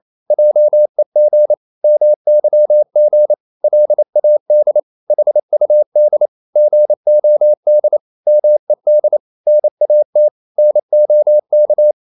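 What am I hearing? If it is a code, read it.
Morse code, heard as a keyed tone, about 28 wpm, 600 Hz. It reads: JEG MYG LAD HUD GOD MED NAT NOK
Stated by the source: Danish short words sent in Morse at 28 wpm, from CW med Gnister.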